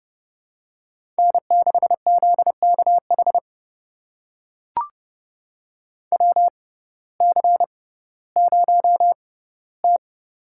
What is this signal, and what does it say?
Morse code, heard as a keyed tone, about 32 wpm, spaced as for 17 wpm, 700 Hz. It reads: N6ZKH WC0T
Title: USA Call Signs - 6x Character Spacing - 30wpm